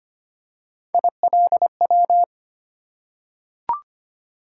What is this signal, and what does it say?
Morse code, heard as a keyed tone, about 25 wpm, 700 Hz. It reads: ILW
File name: Three Letter-Number Combinations - 25wpm